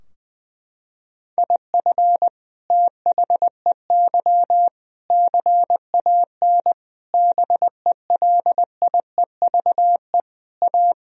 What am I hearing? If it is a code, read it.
Morse code, heard as a keyed tone, about 21 wpm, 700 Hz. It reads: IF THEY CAN BELIEVE A
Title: Sentences from Top 1000 Words - Review Part 2 - 20wpm